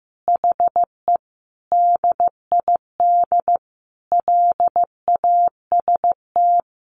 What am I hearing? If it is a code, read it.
Morse code, heard as a keyed tone, about 15 wpm, 700 Hz. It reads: HE DID LAST
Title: Sentences from Top 300 Words 15wpm